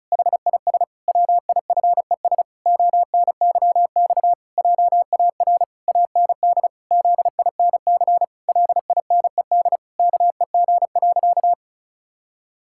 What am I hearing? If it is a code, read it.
Morse code, heard as a keyed tone, about 34 wpm, 700 Hz. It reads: HIS WIFES ONYX JAR AND ZINC LINED KEG.
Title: Pangram Challenge v2 - 35wpm